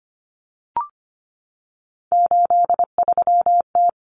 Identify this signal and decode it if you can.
Morse code, heard as a keyed tone, about 25 wpm, 700 Hz. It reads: 83T
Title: Common Cut Numbers - 3-Digit 25wpm